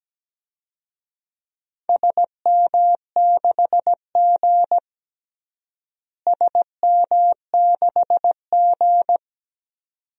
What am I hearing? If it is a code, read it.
Morse code, heard as a keyed tone, about 17 wpm, 700 Hz. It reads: SM6G SM6G
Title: DXCC-100 Call Signs - 4 Characters - Repeated 2x - 17wpm